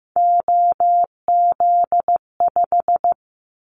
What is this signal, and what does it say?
Morse code, heard as a keyed tone, about 15 wpm, 700 Hz. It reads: OZ5
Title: Callsign Prefixes for Top 100 DXCC - 15wpm